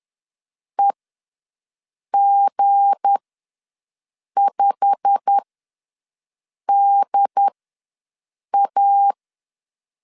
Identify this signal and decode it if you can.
Morse code, heard as a keyed tone, about 10 wpm, spaced as for 5 wpm, 800 Hz. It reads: EG5DA
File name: LEZIONI / 10 SIGNS / 10A SLASH ws 10wpm 4121